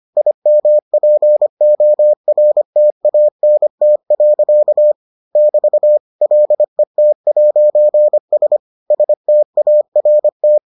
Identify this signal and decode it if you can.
Morse code, heard as a keyed tone, about 25 wpm, 600 Hz. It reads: IMPORTANT. <BT> LET'S START